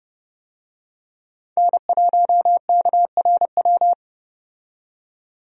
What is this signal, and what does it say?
Morse code, heard as a keyed tone, about 31 wpm, 700 Hz. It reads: N1KRW
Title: USA Call Signs - 5 Characters - 30wpm